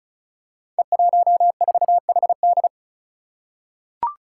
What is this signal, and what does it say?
Morse code, heard as a keyed tone, about 36 wpm, 700 Hz. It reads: E14HD
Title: Five Challenging Letter-Number Combinations 35wpm